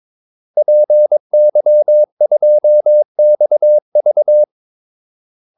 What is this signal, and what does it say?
Morse code, heard as a keyed tone, about 22 wpm, 600 Hz. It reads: PY2XV
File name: Callsigns-22wpm